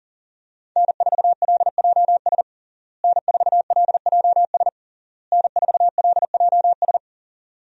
Morse code, heard as a keyed tone, about 38 wpm, 700 Hz. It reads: N4LJS N4LJS N4LJS
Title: USA Call Signs - 5 Characters - Repeated 3x - 40wpm